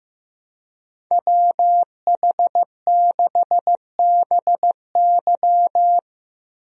Morse code, heard as a keyed tone, about 16 wpm, 700 Hz. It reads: WH6BY
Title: USA Call Signs - 5 Characters - Repeated 2x - 15wpm